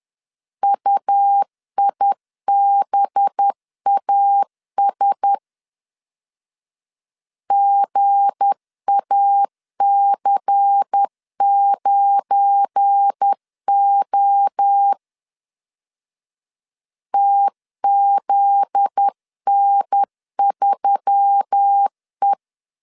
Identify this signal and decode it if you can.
Morse code, heard as a keyed tone, about 10 wpm, 800 Hz. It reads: UIBAS GAC9O TZN3E
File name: LEZIONI / 7 GF9Z / 7 GF9Z 10wpm 2611